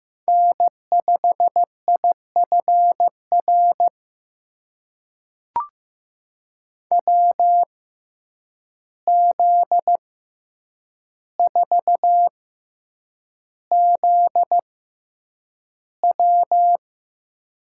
Morse code, heard as a keyed tone, about 15 wpm, spaced as for 8 wpm, 700 Hz. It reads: N5IFR WZ4ZW
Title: USA Call Signs - 5 Characters - 6x Character Spacing - 15wpm